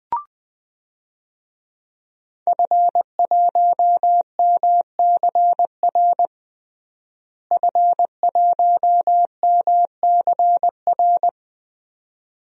Morse code, heard as a keyed tone, about 20 wpm, 700 Hz. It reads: F1MCR F1MCR
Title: DXCC-100 Call Signs - 5 Characters - Repeated 2x - 20wpm